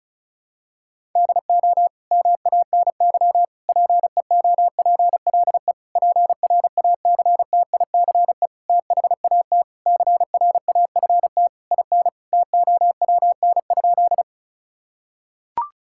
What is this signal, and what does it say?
Morse code, heard as a keyed tone, about 36 wpm, 700 Hz. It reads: DO MANY PEOPLE PRACTICE THAT CRAFT IN TOWN?
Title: Sentences from Top 1000 Words - Review Part 6 - 35wpm